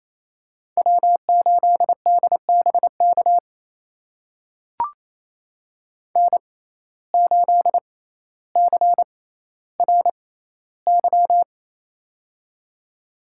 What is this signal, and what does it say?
Morse code, heard as a keyed tone, about 28 wpm, spaced as for 14 wpm, 700 Hz. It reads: W8DBK N8CRY
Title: USA Call Signs - 6x Character Spacing - 28wpm